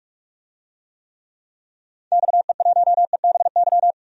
Morse code, heard as a keyed tone, about 47 wpm, 700 Hz. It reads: XE1EBY
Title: DXCC-100 Call Signs - 6 Characters - 45wpm